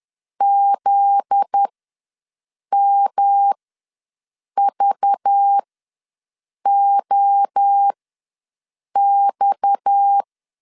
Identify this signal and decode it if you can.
Morse code, heard as a keyed tone, about 10 wpm, spaced as for 6 wpm, 800 Hz. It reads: ZMVOX
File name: LEZIONI / 9 28 ws 10wpm 4124